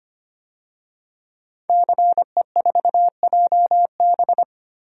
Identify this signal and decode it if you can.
Morse code, heard as a keyed tone, about 25 wpm, 700 Hz. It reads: CE4JB
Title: DXCC-100 Call Signs - 5 Characters - 25wpm